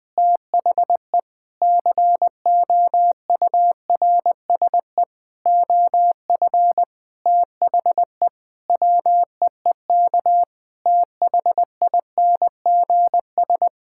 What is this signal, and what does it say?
Morse code, heard as a keyed tone, about 20 wpm, 700 Hz. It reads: THE COURSE OF THE WEEK THINGS
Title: Sentences with 4-Word Phrases - 20wpm